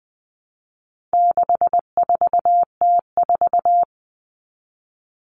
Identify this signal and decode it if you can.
Morse code, heard as a keyed tone, about 20 wpm, 700 Hz. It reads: 64T4
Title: All Cut Numbers - 4-Digit 20wpm